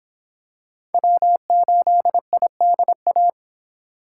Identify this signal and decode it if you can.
Morse code, heard as a keyed tone, about 26 wpm, 700 Hz. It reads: W8IDA